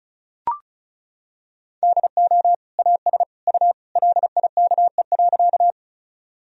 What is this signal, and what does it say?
Morse code, heard as a keyed tone, about 34 wpm, 700 Hz. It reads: DO AS U LIKE.
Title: Sentences from Top 100 Words and Abbreviations 35wpm